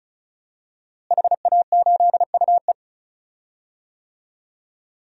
Morse code, heard as a keyed tone, about 36 wpm, 700 Hz. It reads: HA8UE
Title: DXCC-100 Call Signs - 5 Characters - Repeated 2x - 35wpm